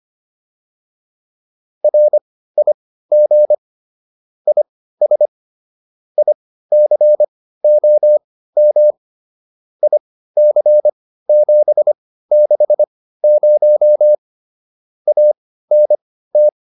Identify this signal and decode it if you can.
Morse code, heard as a keyed tone, about 25 wpm, spaced as for 15 wpm, 600 Hz. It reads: RIG IS ICOM IC760 ANT